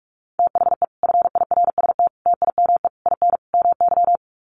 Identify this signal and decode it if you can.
Morse code, heard as a keyed tone, about 44 wpm, 700 Hz. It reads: THE FIRST TIME IN MY
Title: Sets of 5 Words Encore 45wpm